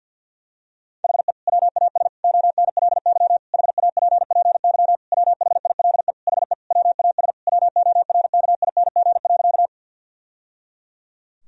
Morse code, heard as a keyed tone, about 50 wpm, 700 Hz. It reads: HE WAS ONLY HAPPY WHILE HE WAS WORKING.
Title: Sentences from Top 1000 Words - Review Part 4 - 50wpm